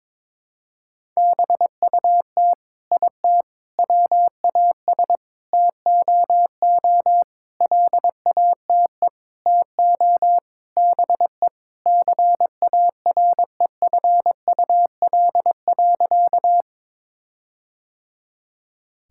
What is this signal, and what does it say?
Morse code, heard as a keyed tone, about 22 wpm, 700 Hz. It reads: BUT IT WAS TOO LATE TO BE CAREFUL.
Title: Sentences from Top 1600 Words 22wpm